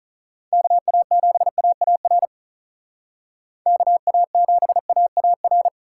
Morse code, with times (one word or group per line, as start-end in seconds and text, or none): 0.5-2.3 KA7AAR
3.7-5.7 KA7AAR